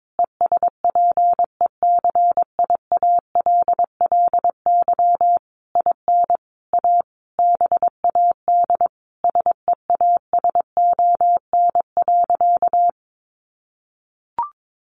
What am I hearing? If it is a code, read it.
Morse code, heard as a keyed tone, about 22 wpm, 700 Hz. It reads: ESPECIALLY IN A BAD SEASON.
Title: Sentences from Top 1400 Words 22wpm